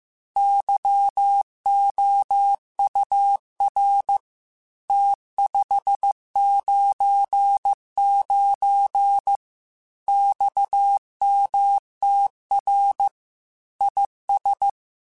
Morse code, heard as a keyed tone, about 15 wpm, 800 Hz. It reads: YOUR T599 XMTR IS
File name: QSOs / QSOs-mp3s / WA7QOX 1@13